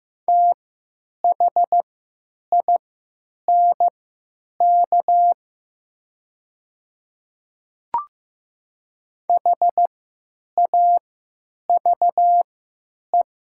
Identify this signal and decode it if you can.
Morse code, heard as a keyed tone, about 15 wpm, spaced as for 9 wpm, 700 Hz. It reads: THINK HAVE
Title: 3x Character Spacing - Top 300 Words - 15wpm